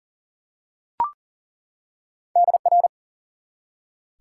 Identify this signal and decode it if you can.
Morse code, heard as a keyed tone, about 40 wpm, 700 Hz. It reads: DR
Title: Top 90 QSO Elements 40wpm